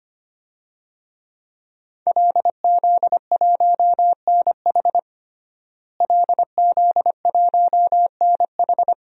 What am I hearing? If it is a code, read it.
Morse code, heard as a keyed tone, about 25 wpm, 700 Hz. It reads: LZ1NH LZ1NH
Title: DXCC-100 Call Signs - 5 Characters - Repeated 2x - 25wpm